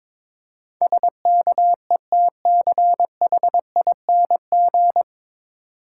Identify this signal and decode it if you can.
Morse code, heard as a keyed tone, about 24 wpm, 690 Hz. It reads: SKETCHING
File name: Easy Words - Speed-Racing with Hints - 22wpm